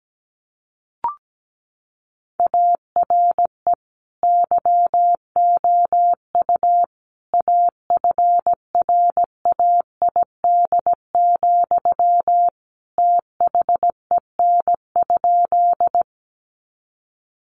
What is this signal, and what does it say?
Morse code, heard as a keyed tone, about 17 wpm, 700 Hz. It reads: ARE YOU AFRAID, THEN?
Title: Sentences from Top 1500 Words 17wpm